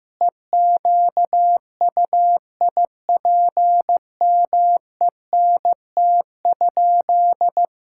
Morse code, heard as a keyed tone, about 16 wpm, 700 Hz. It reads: EQUIPMENT?
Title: Sentences from Top 1000 Words - Review Part 5 - 15wpm